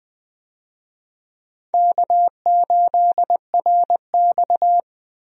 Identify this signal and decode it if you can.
Morse code, heard as a keyed tone, about 20 wpm, 700 Hz. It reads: K8RX
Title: USA Call Signs - 4 Characters - Repeated 2x - 20wpm